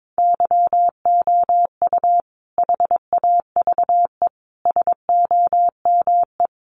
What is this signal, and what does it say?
Morse code, heard as a keyed tone, about 22 wpm, 700 Hz. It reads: YOU HAVE SOME
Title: Sentences from Top 1300 Words 22wpm